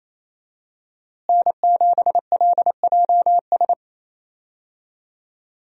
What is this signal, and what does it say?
Morse code, heard as a keyed tone, about 28 wpm, 700 Hz. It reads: N7LJS
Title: USA Call Signs - 5 Characters - Repeated 2x - 28wpm